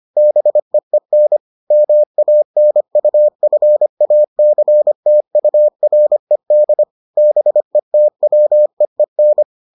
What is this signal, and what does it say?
Morse code, heard as a keyed tone, about 25 wpm, 600 Hz. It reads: BEEN MANUFACTURED BETWEEN